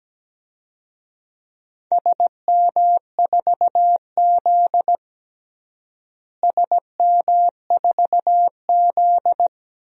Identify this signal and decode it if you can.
Morse code, heard as a keyed tone, about 17 wpm, 700 Hz. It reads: SM4Z SM4Z
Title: DXCC-100 Call Signs - 4 Characters - Repeated 2x - 17wpm